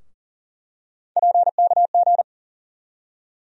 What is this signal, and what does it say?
Morse code, heard as a keyed tone, about 40 wpm, 700 Hz. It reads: PKG
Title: USA Call Sign Postfixes - 3 Letters 40wpm